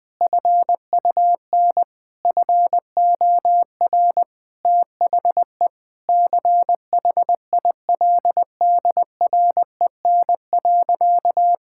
Morse code, heard as a keyed tone, about 20 wpm, 700 Hz. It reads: FUN FOR THE CHILDREN.
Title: Sentences from Top 1000 Words - Review Part 5 - 20wpm